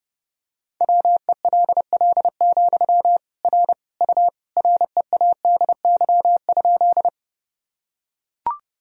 Morse code, heard as a keyed tone, about 30 wpm, 700 Hz. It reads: WELL, R U READY?
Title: Sentences from Top 400 Words and Abbreviations 30wpm